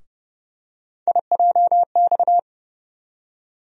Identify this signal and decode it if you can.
Morse code, heard as a keyed tone, about 30 wpm, 700 Hz. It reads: IJX
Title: USA Call Sign Postfixes - 3 Letters 30wpm